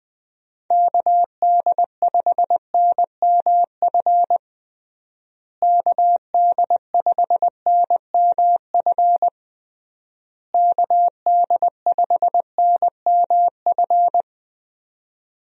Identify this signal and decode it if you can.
Morse code, heard as a keyed tone, about 20 wpm, 700 Hz. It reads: KD5NMF KD5NMF KD5NMF
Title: USA Call Signs - 6 Characters - Repeated 3x - 20wpm